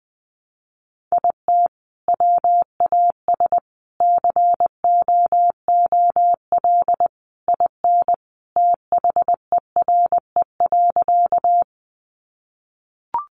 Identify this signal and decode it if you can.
Morse code, heard as a keyed tone, about 20 wpm, 700 Hz. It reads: IT WAS COOL IN THERE.
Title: Sentences from Top 1200 Words 20wpm